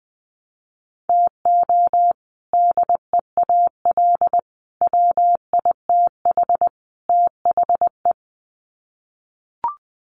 Morse code, heard as a keyed tone, about 20 wpm, 700 Hz. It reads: TO DEAL WITH THE
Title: Sets of 4 Words 20wpm